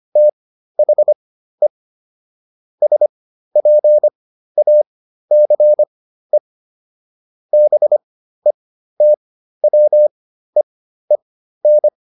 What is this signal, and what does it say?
Morse code, heard as a keyed tone, about 25 wpm, spaced as for 13 wpm, 600 Hz. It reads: THE SPACE BETWEEN